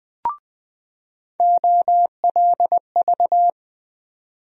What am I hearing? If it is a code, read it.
Morse code, heard as a keyed tone, about 20 wpm, 700 Hz. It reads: OLV
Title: USA Call Sign Postfixes - 3 Letters 20wpm